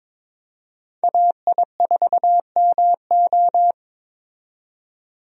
Morse code, heard as a keyed tone, about 21 wpm, 700 Hz. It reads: AI4MO
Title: USA Call Signs - 5 Characters - 22wpm